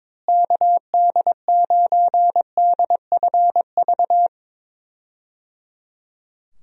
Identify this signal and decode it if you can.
Morse code, heard as a keyed tone, about 22 wpm, 700 Hz. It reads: KD9DFV
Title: USA Call Signs - 6 Characters - Repeated 3x - 22wpm